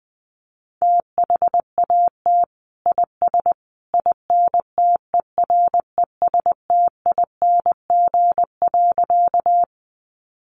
Morse code, heard as a keyed tone, about 20 wpm, 700 Hz. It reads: THAT IS INTERESTING.
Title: Sentences from Top 1000 Words 20wpm